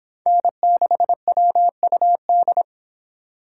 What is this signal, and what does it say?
Morse code, heard as a keyed tone, about 31 wpm, 700 Hz. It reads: N6WUD